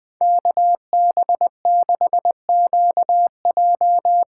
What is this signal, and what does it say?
Morse code, heard as a keyed tone, about 19 wpm, 700 Hz. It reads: KB6QJ
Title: Speed Racing - USA 2x2 Call Signs 17wpm